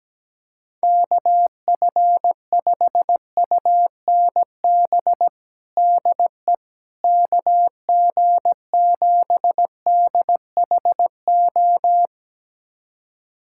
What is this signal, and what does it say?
Morse code, heard as a keyed tone, about 17 wpm, 700 Hz. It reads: KF5UNB DE KG7DHO
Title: USA Callsign DE Callsign - 6 Characters - 17wpm